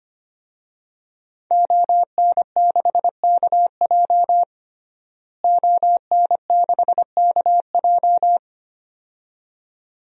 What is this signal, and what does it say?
Morse code, heard as a keyed tone, about 25 wpm, 700 Hz. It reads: ON6KJ ON6KJ
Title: DXCC-100 Call Signs - 5 Characters - Repeated 2x - 25wpm